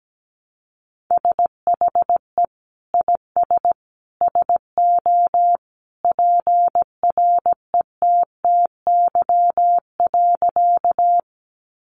Morse code, heard as a keyed tone, about 17 wpm, 700 Hz. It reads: SHE IS SO PRETTY.
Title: Sentences from Top 1000 Words 17wpm